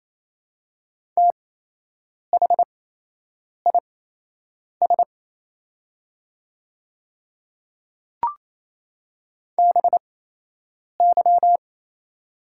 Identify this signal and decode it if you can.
Morse code, heard as a keyed tone, about 27 wpm, spaced as for 8 wpm, 690 Hz. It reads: THIS BY
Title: 8x Character Spacing - Top 100 Words - 28wpm